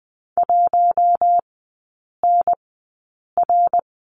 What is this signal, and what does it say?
Morse code, heard as a keyed tone, about 20 wpm, spaced as for 8 wpm, 700 Hz. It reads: 1NR
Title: 087 Lesson 15e - Number 5 - Three Characters 20wpm